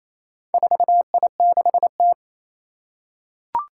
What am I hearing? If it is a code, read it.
Morse code, heard as a keyed tone, about 28 wpm, 700 Hz. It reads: 4I6T
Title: Four Challenging Letter-Number Combinations 28wpm